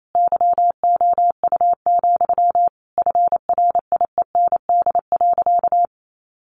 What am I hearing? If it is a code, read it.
Morse code, heard as a keyed tone, about 28 wpm, 700 Hz. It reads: YOU, FRIEND.
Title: Sentences from Top 1500 Words 28wpm